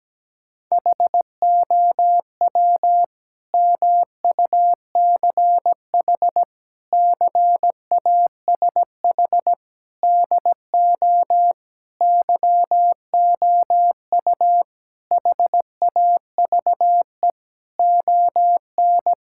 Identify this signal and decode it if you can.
Morse code, heard as a keyed tone, about 17 wpm, 700 Hz. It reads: HOW MUCH CASH DO YOU HAVE ON
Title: Sentences from Top 1000 Words - Review Part 5 - 17wpm